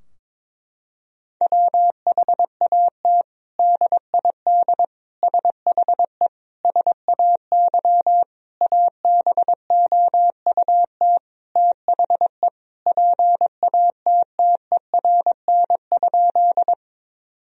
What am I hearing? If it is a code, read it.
Morse code, heard as a keyed tone, about 22 wpm, 700 Hz. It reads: WHAT DID SHE SAY ABOUT THE PATTERN?
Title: Sentences from Top 1000 Words - Review Part 2 - 22wpm